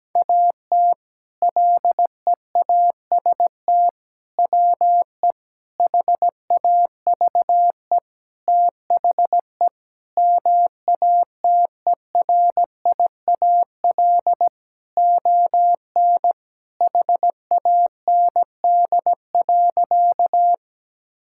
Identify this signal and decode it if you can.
Morse code, heard as a keyed tone, about 17 wpm, 700 Hz. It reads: AT LEAST WE HAVE THE MATERIAL ON HAND.
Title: Sentences from Top 1000 Words - Review Part 5 - 17wpm